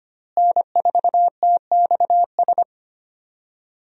Morse code, heard as a keyed tone, about 25 wpm, 700 Hz. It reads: N4TXS
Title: USA Call Signs - 5 Characters - Repeated 3x - 25wpm